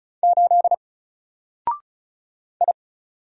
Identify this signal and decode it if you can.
Morse code, heard as a keyed tone, about 34 wpm, 700 Hz. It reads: 8 I
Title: Single Letter-Number 35wpm